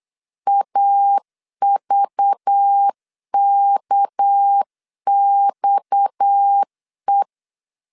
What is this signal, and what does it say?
Morse code, heard as a keyed tone, about 8 wpm, 800 Hz. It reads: AVKXE
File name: CW / DET / 1 9 1500C S A 8wpm 5637